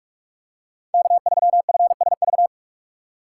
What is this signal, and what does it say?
Morse code, heard as a keyed tone, about 46 wpm, 700 Hz. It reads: K3FSV
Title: USA Call Signs - 2x Character Spacing - 45wpm